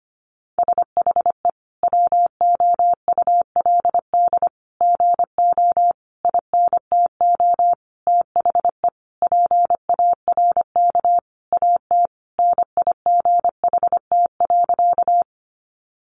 Morse code, heard as a keyed tone, about 25 wpm, 700 Hz. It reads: SHE WOULD GO INTO THE PARK AT NIGHT.
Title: Sentences from Top 1300 Words 25wpm